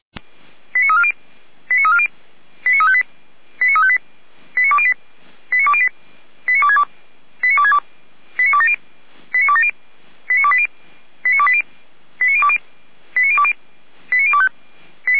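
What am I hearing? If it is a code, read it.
Five-tone selective call (ZVEI), heard as quick runs of five tones, tones of about 70 ms. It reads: 89340 89340 89358 89358 89208 89208 89262 89262 89270 89270 89280 89280 89020 89020 89025
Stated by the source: ZVEI five-tone selective call received on SDR.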